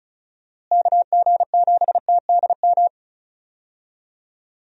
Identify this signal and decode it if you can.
Morse code, heard as a keyed tone, about 34 wpm, 700 Hz. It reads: KG7TDM
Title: USA Call Signs - 6 Characters - Repeated 2x - 35wpm